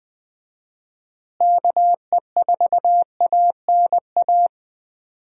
Morse code, heard as a keyed tone, about 20 wpm, 700 Hz. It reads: KE4ANA
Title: USA Call Signs - 6 Characters - Repeated 3x - 20wpm